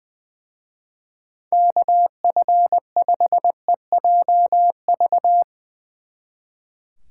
Morse code, heard as a keyed tone, about 22 wpm, 700 Hz. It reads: KF5EJV
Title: Speed Racing - USA 2x3 Call Signs 20wpm